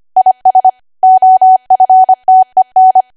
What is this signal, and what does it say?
Morse code, heard as a keyed tone, about 25 wpm, 750 Hz. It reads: IS OFTEN
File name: Archive / 25 WPM / 25WPM